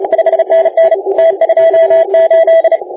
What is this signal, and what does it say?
Morse code, heard as a keyed tone, about 31 wpm, 660 Hz, with noise. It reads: 5NN T28
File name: la4c / sac-cw-2018 / CONTEST AUDIO 1600Z